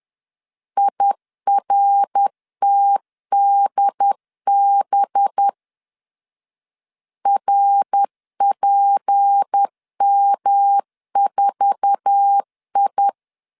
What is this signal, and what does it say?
Morse code, heard as a keyed tone, about 10 wpm, 800 Hz. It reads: IRTDB RPM4I
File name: LEZIONI / 8 YJ45 / 8 YJ4 10wpm 2633